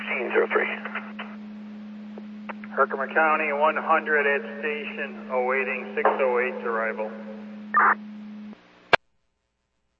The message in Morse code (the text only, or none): none